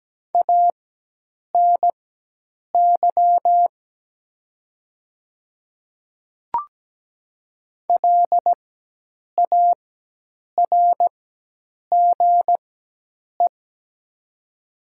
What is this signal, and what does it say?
Morse code, heard as a keyed tone, about 17 wpm, spaced as for 8 wpm, 700 Hz. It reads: ANY LARGE
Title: 4x Character Spacing - Top 200 Words - 17wpm